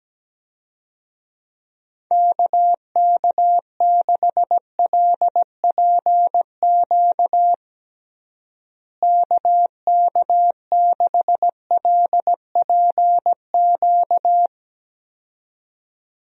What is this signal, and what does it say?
Morse code, heard as a keyed tone, about 17 wpm, 700 Hz. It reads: KK6LPQ KK6LPQ
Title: USA Call Signs - 6 Characters - Repeated 2x - 17wpm